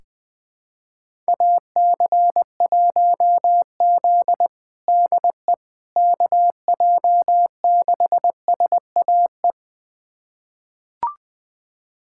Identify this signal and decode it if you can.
Morse code, heard as a keyed tone, about 20 wpm, 700 Hz. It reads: AC1Z DE KJ6SAE
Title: USA Callsign DE Callsign 20wpm